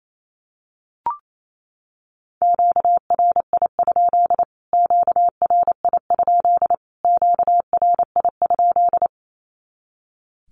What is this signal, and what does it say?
Morse code, heard as a keyed tone, about 28 wpm, 700 Hz. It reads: QRI? QRI? QRI?
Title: Q-Codes Repeated 3x 28wpm